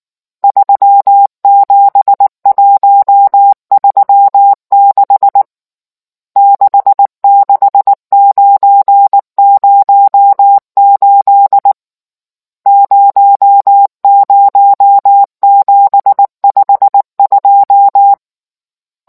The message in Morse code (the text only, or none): 37136 66908 00752